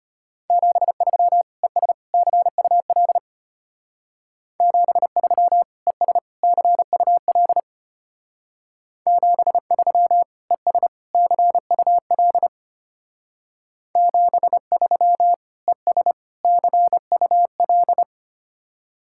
Morse code, heard as a keyed tone, about 32 wpm, 700 Hz. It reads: 73 ES CUL 73 ES CUL 73 ES CUL 73 ES CUL